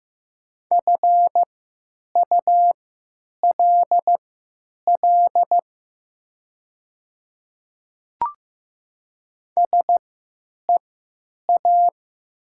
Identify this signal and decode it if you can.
Morse code, heard as a keyed tone, about 15 wpm, spaced as for 9 wpm, 700 Hz. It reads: FULL SEA
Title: 3x Character Spacing - Top 500 Words - 15wpm